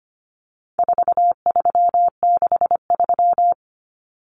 Morse code, heard as a keyed tone, about 25 wpm, 700 Hz. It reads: 4363
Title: Common Cut Numbers - 4-Digit 25wpm